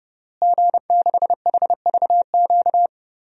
Morse code, heard as a keyed tone, about 30 wpm, 700 Hz. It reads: G6HVQ
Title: DXCC-100 Call Signs - 5 Characters - Repeated 3x - 30wpm